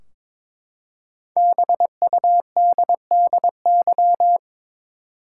Morse code, heard as a keyed tone, about 22 wpm, 700 Hz. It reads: BUDDY
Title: Five Letter Words - 22wpm